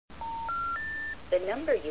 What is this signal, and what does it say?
Special information tone, with speech.